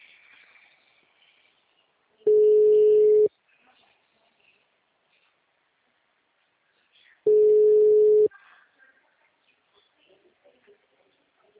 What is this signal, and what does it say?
Ringback tone (Europe, 425 Hz).
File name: out- -2022- - - .1004